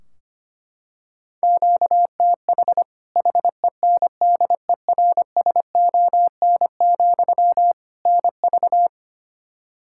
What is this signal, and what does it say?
Morse code, heard as a keyed tone, about 25 wpm, 700 Hz. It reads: QTH HENDERSON, NV
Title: Practice QSO - Exchange 1 - Beginning - 25wpm